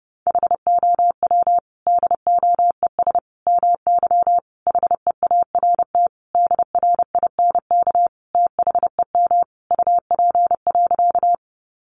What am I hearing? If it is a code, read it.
Morse code, heard as a keyed tone, about 30 wpm, 700 Hz. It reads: HOW DOES MY HEART DRINK THEM UP.
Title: Sentences from Top 1500 Words 30wpm